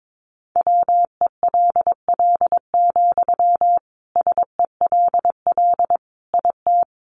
Morse code, heard as a keyed tone, about 22 wpm, 700 Hz. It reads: WELL, SELL IT
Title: Sentences from Top 1100 Words 22wpm